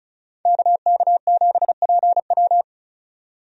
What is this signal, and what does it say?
Morse code, heard as a keyed tone, about 36 wpm, 700 Hz. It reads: KK7PW
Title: USA Call Signs - 5 Characters - 35wpm